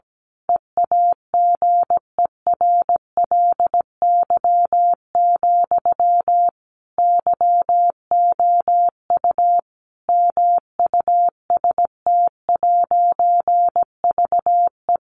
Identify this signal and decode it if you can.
Morse code, heard as a keyed tone, about 17 wpm, 700 Hz. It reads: EAGERLY, YOU MUST'VE
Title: A Sound Of Thunder 17wpm